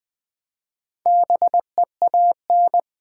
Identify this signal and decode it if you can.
Morse code, heard as a keyed tone, about 20 wpm, 700 Hz. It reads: BEAN
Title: Build A Word - 1 Letter At A Time - 4-Letter Words - 20wpm